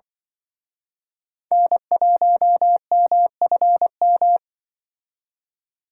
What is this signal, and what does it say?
Morse code, heard as a keyed tone, about 25 wpm, 700 Hz. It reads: N1MFM
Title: Speed Racing - USA 1x3 Call Signs 20wpm